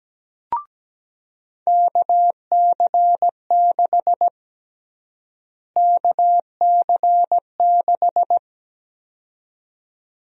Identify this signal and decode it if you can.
Morse code, heard as a keyed tone, about 17 wpm, 700 Hz. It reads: KC6 KC6
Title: USA Call Sign Prefixes - Repeated 2x - 17wpm